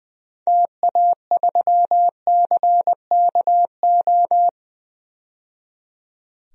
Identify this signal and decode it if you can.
Morse code, heard as a keyed tone, about 21 wpm, 700 Hz. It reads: TA3CKO
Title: DXCC-100 Call Signs - 6 Characters - 20wpm